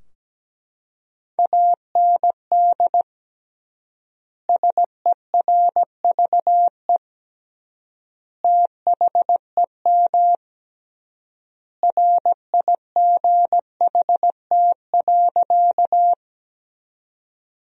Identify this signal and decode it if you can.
Morse code, heard as a keyed tone, about 17 wpm, 700 Hz. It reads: AND SERVE THEM RIGHT.
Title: 3x Word-Spacing - Sentences from Top 400 Words - 17wpm